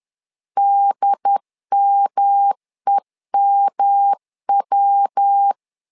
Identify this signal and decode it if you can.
Morse code, heard as a keyed tone, about 10 wpm, 800 Hz. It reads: DMEMW
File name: LEZIONI / 3 MWRO / 3 MW 10wpm 1959